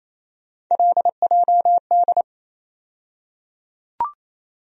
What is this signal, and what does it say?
Morse code, heard as a keyed tone, about 28 wpm, 700 Hz. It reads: LJD
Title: USA Call Sign Postfixes - 3 Letters 28wpm